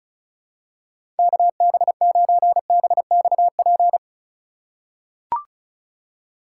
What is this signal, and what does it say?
Morse code, heard as a keyed tone, about 36 wpm, 700 Hz. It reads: KB9BXP
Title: USA Call Signs - 6 Characters - 2x Character Spacing - 35wpm